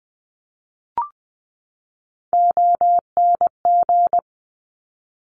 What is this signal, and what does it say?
Morse code, heard as a keyed tone, about 20 wpm, 700 Hz. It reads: ONG
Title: Top Three Letter Combinations in Words 20wpm